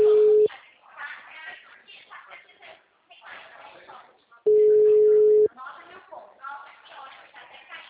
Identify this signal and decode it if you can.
Ringback tone (Europe, 425 Hz), with speech.